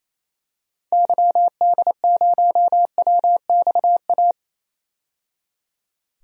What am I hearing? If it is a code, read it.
Morse code, heard as a keyed tone, about 27 wpm, 700 Hz. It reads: YD0WXA
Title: DXCC-100 Call Signs - 6 Characters - 28wpm